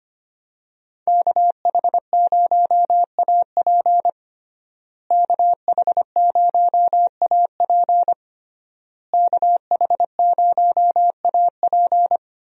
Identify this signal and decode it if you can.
Morse code, heard as a keyed tone, about 25 wpm, 700 Hz. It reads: KH0AP KH0AP KH0AP
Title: USA Call Signs - 5 Characters - Repeated 3x - 25wpm